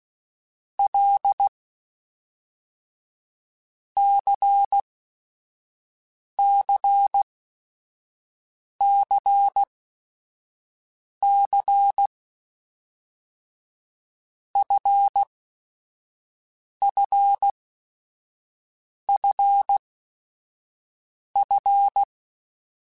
Morse code, heard as a keyed tone, about 16 wpm, 800 Hz. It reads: L C C C C F F F F
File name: Basic Sets / Basic Letter Groups Collective7 / LFCYQ@5